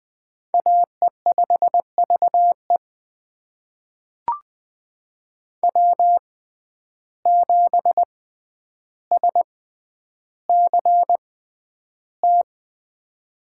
Morse code, heard as a keyed tone, about 20 wpm, spaced as for 10 wpm, 700 Hz. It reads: AE5VE W7SCT
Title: USA Call Signs - 5 Characters - 6x Character Spacing - 20wpm